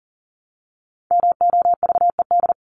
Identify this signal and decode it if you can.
Morse code, heard as a keyed tone, about 40 wpm, 700 Hz. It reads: MOVED
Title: ED Suffix - Top 100 Words with Base Words 40wpm